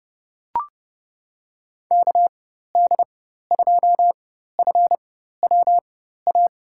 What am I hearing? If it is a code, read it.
Morse code, heard as a keyed tone, about 30 wpm, spaced as for 14 wpm, 700 Hz. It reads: KD2FWA
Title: USA Call Signs - 4x Character Spacing - 30wpm